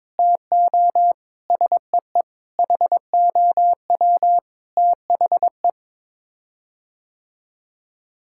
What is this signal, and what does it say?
Morse code, heard as a keyed tone, about 22 wpm, 700 Hz. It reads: TO SEE HOW THE
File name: practice / 22wpm / Sets of 8 Words - 22wpm